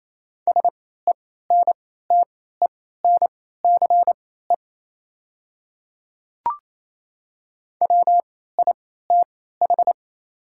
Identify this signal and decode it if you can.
Morse code, heard as a keyed tone, about 28 wpm, spaced as for 16 wpm, 700 Hz. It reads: SENTENCE WITH
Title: 3x Character Spacing - Top 500 Words - 28wpm